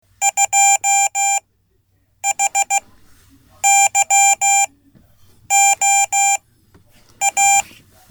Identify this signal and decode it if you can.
Morse code, heard as a keyed tone, about 16 wpm, spaced as for 8 wpm, 790 Hz. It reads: 2HYOA
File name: Code / 5min-b